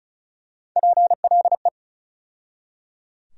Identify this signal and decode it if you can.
Morse code, heard as a keyed tone, about 34 wpm, 700 Hz. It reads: PLE
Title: USA Call Sign Postfixes - 3 Letters 35wpm